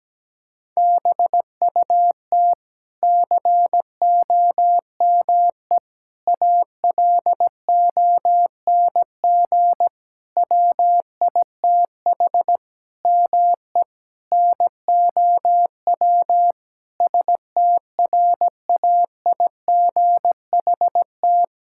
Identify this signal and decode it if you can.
Morse code, heard as a keyed tone, about 17 wpm, 700 Hz. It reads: BUT COME ALONG WITH ME NOW STRAIGHT
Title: Sentences from Top 1000 Words - Review Part 6 - 17wpm